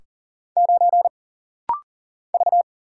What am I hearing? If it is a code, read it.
Morse code, heard as a keyed tone, about 40 wpm, 700 Hz. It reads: 9 V
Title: Single Letter-Number - Rapid-Fire 40wpm